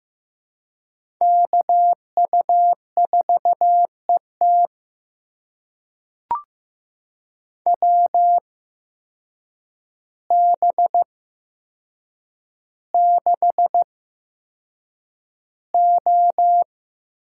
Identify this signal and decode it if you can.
Morse code, heard as a keyed tone, about 15 wpm, spaced as for 7 wpm, 700 Hz. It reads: KU4ET WB6O
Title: USA Call Signs - 8x Character Spacing - 15wpm